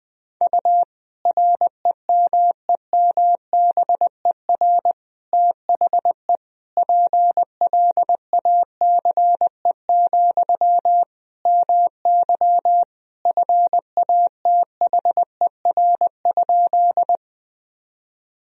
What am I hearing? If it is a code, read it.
Morse code, heard as a keyed tone, about 21 wpm, 700 Hz. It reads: U REMEMBER THE PLACE, MY FATHER?
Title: Sentences from Top 400 Words and Abbreviations 20wpm